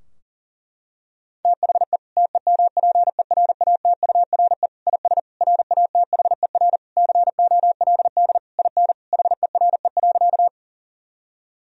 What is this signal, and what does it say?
Morse code, heard as a keyed tone, about 40 wpm, 700 Hz. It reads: THE TEMPERATURE IS RATHER COLD IN HERE.
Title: Sentences from Top 1000 Words - Review Part 4 - 40wpm